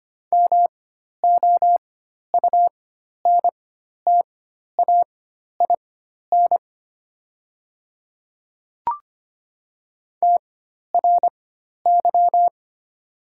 Morse code, heard as a keyed tone, about 25 wpm, spaced as for 12 wpm, 690 Hz. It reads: MOUNTAIN TRY
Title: 4x Character Spacing - Top 300 Words - 25wpm